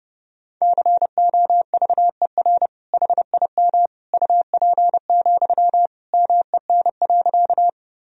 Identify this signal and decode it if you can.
Morse code, heard as a keyed tone, about 30 wpm, 700 Hz. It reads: COVER HIM UP, MEN.
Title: Sentences from Top 1000 Words - Review Part 1 - 30wpm